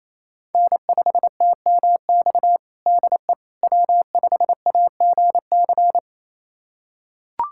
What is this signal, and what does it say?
Morse code, heard as a keyed tone, about 27 wpm, 700 Hz. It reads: N5TMX DE W5AGC
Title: USA Callsign DE Callsign - 5 Characters - 28wpm